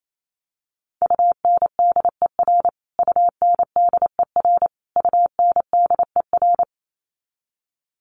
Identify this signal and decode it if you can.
Morse code, heard as a keyed tone, about 28 wpm, 700 Hz. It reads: UNDER UNDER UNDER
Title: Top 300 Words Repeated 28wpm